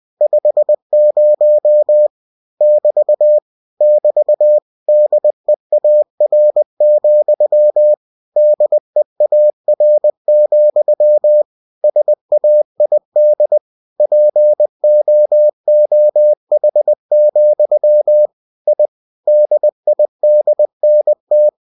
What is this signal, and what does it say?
Morse code, heard as a keyed tone, about 20 wpm, 600 Hz. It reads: 50 <BT> <BT> DEAR, DEAR, SAID POOH, I DIDNT